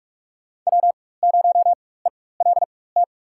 Morse code, heard as a keyed tone, about 41 wpm, spaced as for 20 wpm, 700 Hz. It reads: W0ELT
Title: USA Call Signs - 5 Characters - 4x Character Spacing - 45wpm